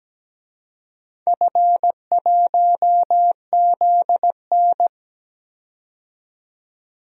Morse code, heard as a keyed tone, about 17 wpm, 700 Hz. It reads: F1ZN
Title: DXCC-100 Call Signs - 4 Characters - 17wpm